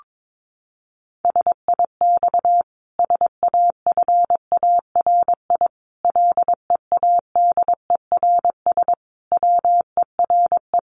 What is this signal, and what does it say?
Morse code, heard as a keyed tone, about 22 wpm, 700 Hz. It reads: SIX SAFARI LEADERS WERE
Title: A Sound Of Thunder 22wpm